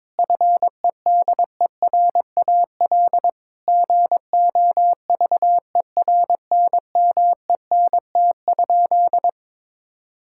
Morse code, heard as a keyed tone, about 22 wpm, 700 Hz. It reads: FEDERAL GOVERNMENT?
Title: Sentences from Top 1000 Words - Review Part 5 - 22wpm